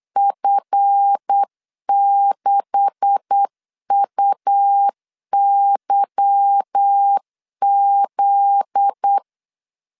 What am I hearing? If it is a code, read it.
Morse code, heard as a keyed tone, about 8 wpm, 800 Hz. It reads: F6UYZ
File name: CW / DET / 1 9 1500C S A 8wpm 5637